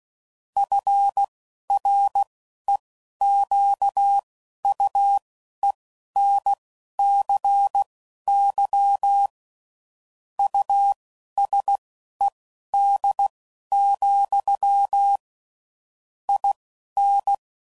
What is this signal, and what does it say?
Morse code, heard as a keyed tone, about 16 wpm, spaced as for 11 wpm, 800 Hz. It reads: FREQUENCY USED, IN
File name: CW Code Groups @10WPM / Technical Subjects / SwPwrSup